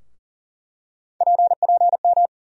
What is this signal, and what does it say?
Morse code, heard as a keyed tone, about 40 wpm, 700 Hz. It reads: PPM
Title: USA Call Sign Postfixes - 3 Letters 40wpm